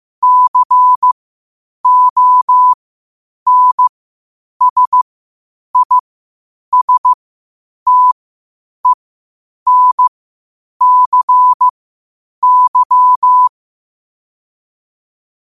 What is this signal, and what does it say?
Morse code, heard as a keyed tone, about 15 wpm, spaced as for 8 wpm, 1000 Hz. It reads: CONSISTENCY
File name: qotdfiles / 8wpm